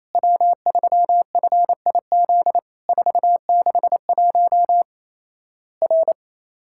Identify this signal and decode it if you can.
Morse code, heard as a keyed tone, about 27 wpm, 700 Hz. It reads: W3FIZ 461 R
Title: Worked All Europe DX Contest - QTC Exchange - 28wpm